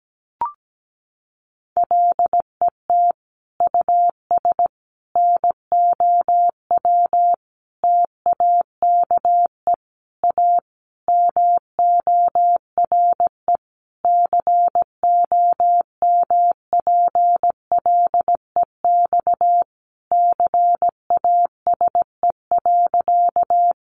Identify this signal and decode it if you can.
Morse code, heard as a keyed tone, about 17 wpm, 700 Hz. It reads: LET US NOW TAKE A MORE COMPLEX CASE.
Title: Sentences from Top 900 Words 17wpm